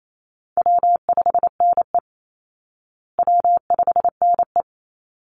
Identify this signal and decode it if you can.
Morse code, heard as a keyed tone, about 28 wpm, 700 Hz. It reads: W5NE W5NE
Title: Easy Call Signs Repeated 2x - 28wpm